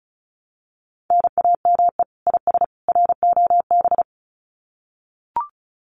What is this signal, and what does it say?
Morse code, heard as a keyed tone, about 34 wpm, 700 Hz. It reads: NAME IS ROB
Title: Common QSO Phrases 35wpm